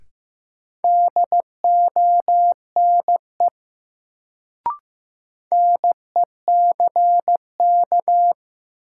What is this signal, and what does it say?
Morse code, heard as a keyed tone, about 16 wpm, 700 Hz. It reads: DONE NECK
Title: Sets of 5 Words Encore - New Word Drill - 15wpm